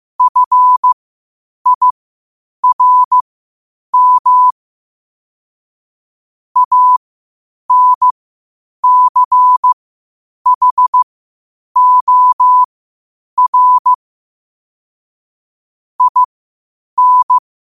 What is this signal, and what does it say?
Morse code, heard as a keyed tone, about 15 wpm, spaced as for 8 wpm, 1000 Hz. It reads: FIRM ANCHOR IN
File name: qotdfiles / 8wpm